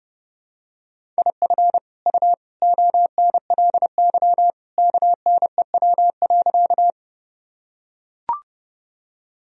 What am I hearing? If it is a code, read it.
Morse code, heard as a keyed tone, about 30 wpm, 700 Hz. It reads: IF U ONLY KNEW.
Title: Sentences from Top 400 Words and Abbreviations 30wpm